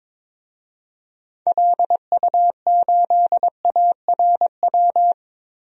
Morse code, heard as a keyed tone, about 23 wpm, 700 Hz. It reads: LU8ARW
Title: DXCC-100 Call Signs - 6 Characters - 22wpm